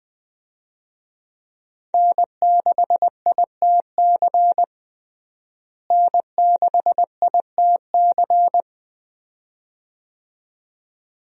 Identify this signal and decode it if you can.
Morse code, heard as a keyed tone, about 21 wpm, 700 Hz. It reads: N6ITC N6ITC
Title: USA Call Signs - 5 Characters - Repeated 2x - 20wpm